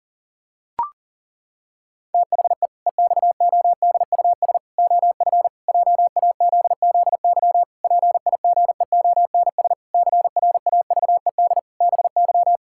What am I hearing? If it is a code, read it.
Morse code, heard as a keyed tone, about 40 wpm, 700 Hz. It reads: THE EXODUS OF JAZZY PIGEONS CRAVED BY
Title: Pangram Challenge v2 - 40wpm